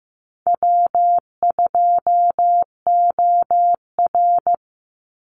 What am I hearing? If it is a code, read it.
Morse code, heard as a keyed tone, about 15 wpm, 700 Hz. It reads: W2OR
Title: Easy Call Signs - 15wpm